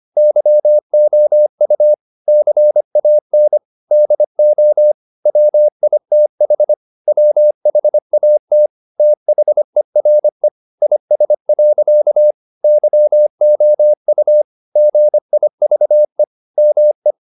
Morse code, heard as a keyed tone, about 25 wpm, 600 Hz. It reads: YOU CAN DO WITH WHAT THERE IS. YOU GIVE ME